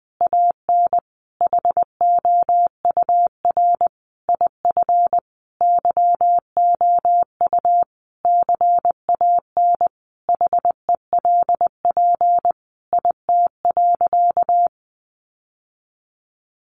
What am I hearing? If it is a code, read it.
Morse code, heard as a keyed tone, about 19 wpm, 700 Hz. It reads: AN HOUR IF YOU CAN HELP IT.
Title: Sentences from Top 1100 Words 20wpm